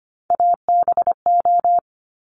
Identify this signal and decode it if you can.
Morse code, heard as a keyed tone, about 25 wpm, 700 Hz. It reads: ABO
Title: Top Three Letter Combinations in Words 25wpm